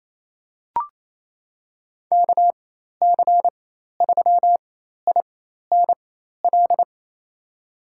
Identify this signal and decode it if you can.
Morse code, heard as a keyed tone, about 27 wpm, spaced as for 13 wpm, 700 Hz. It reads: KC3INL
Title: USA Call Signs - 4x Character Spacing - 28wpm